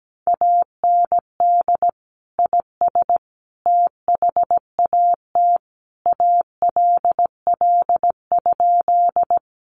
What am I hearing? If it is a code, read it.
Morse code, heard as a keyed tone, about 17 wpm, 700 Hz. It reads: AND IS THAT ALL?
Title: Sentences from Top 100 Words 17wpm